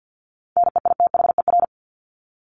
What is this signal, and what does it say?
Morse code, heard as a keyed tone, about 50 wpm, 700 Hz. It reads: NEITHER
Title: Top 1501-1750 Words 50wpm